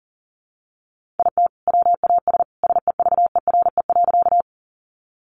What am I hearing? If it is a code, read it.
Morse code, heard as a keyed tone, about 40 wpm, 700 Hz. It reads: IT WAS SEVERE.
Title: Sentences from Top 1400 Words 40wpm